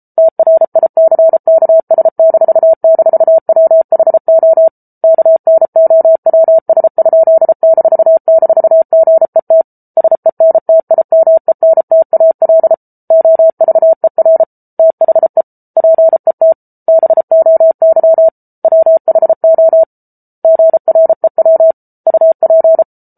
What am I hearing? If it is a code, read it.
Morse code, heard as a keyed tone, about 34 wpm, 650 Hz. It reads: TRICKS--WHO KNOWS?--GET SENTIMENTAL OVER THE PET BOY WHO GREW UP